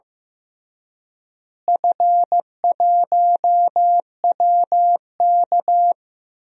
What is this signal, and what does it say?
Morse code, heard as a keyed tone, about 16 wpm, 700 Hz. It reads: F1WK
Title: DXCC-100 Call Signs - 4 Characters - Repeated 3x - 15wpm